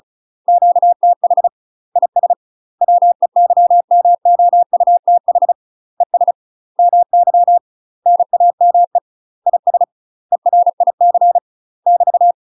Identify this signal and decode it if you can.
Morse code, heard as a keyed tone, about 34 wpm, 700 Hz. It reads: QTH IS WEYMOUTH ES MY NAME IS ERIC <BT>
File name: UK MASTER QSOs TRAINING/35WPM